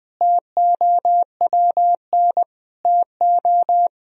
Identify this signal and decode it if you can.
Morse code, heard as a keyed tone, about 20 wpm, 700 Hz. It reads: TOWN TO
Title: Sentences from Top 1000 Words - Review Part 2 - 20wpm